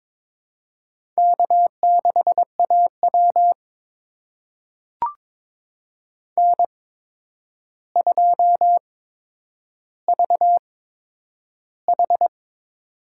Morse code, heard as a keyed tone, about 22 wpm, spaced as for 9 wpm, 700 Hz. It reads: K6AW N2VH
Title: USA Call Signs - 4 Characters - 8x Character Spacing - 22wpm